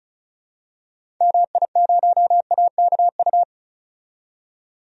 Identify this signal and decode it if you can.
Morse code, heard as a keyed tone, about 34 wpm, 700 Hz. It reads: MI0AKU
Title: DXCC-100 Call Signs - 6 Characters - 35wpm